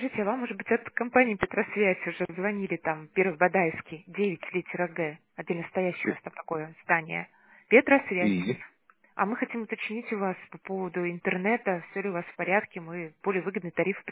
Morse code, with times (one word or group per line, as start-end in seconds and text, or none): none